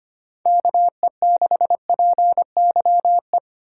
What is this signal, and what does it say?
Morse code, heard as a keyed tone, about 25 wpm, 700 Hz. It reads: KE6PYE
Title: USA Call Signs - 6 Characters - 3x Character Spacing - 25wpm